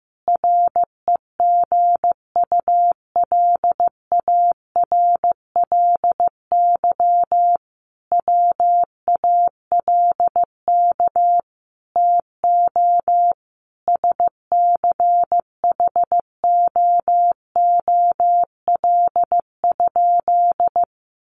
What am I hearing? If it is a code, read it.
Morse code, heard as a keyed tone, about 15 wpm, 700 Hz. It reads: REGULARLY WALK TO SCHOOL?
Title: Sentences from Top 1400 Words 15wpm